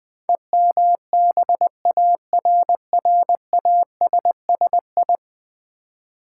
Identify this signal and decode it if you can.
Morse code, heard as a keyed tone, about 20 wpm, 700 Hz. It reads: EMBARRASSI
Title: Build A Word - 5 Letters At A Time - 12-Letter Words - 20wpm